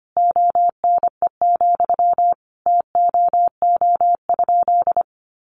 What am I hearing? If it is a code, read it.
Morse code, heard as a keyed tone, about 25 wpm, 700 Hz. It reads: ONE, TOO?
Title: Sentences from Top 200 Words 25wpm